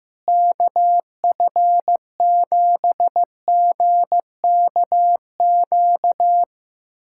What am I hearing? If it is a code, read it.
Morse code, heard as a keyed tone, about 15 wpm, 700 Hz. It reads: KF7GKQ
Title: USA Callsign DE Callsign - 6 Characters - 15wpm